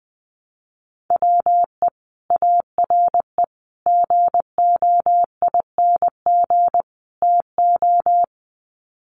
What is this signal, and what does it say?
Morse code, heard as a keyed tone, about 20 wpm, 700 Hz. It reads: WE ARE GOING TO
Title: Sets of 4 Words 20wpm